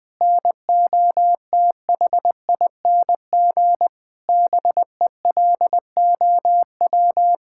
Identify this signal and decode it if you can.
Morse code, heard as a keyed tone, about 20 wpm, 700 Hz. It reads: NOTHING BELOW
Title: Sentences from Top 1000 Words - Review Part 5 - 20wpm